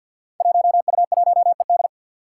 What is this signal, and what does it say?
Morse code, heard as a keyed tone, about 46 wpm, 700 Hz. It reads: 1U1ED
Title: Five Challenging Letter-Number Combinations 50wpm